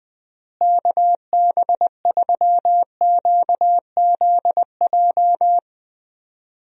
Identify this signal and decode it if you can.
Morse code, heard as a keyed tone, about 21 wpm, 700 Hz. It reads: KB3QZJ
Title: USA Call Signs - 6 Characters - Repeated 3x - 20wpm